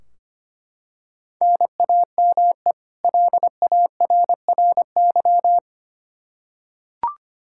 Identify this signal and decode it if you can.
Morse code, heard as a keyed tone, about 25 wpm, 700 Hz. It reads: NAME LARRY
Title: Practice QSO - Exchange 1 - Beginning - 25wpm